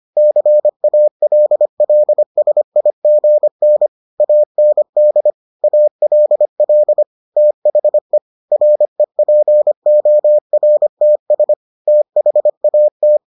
Morse code, heard as a keyed tone, about 25 wpm, 600 Hz. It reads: CALLSIGN AND ALL THE REPORTS THAT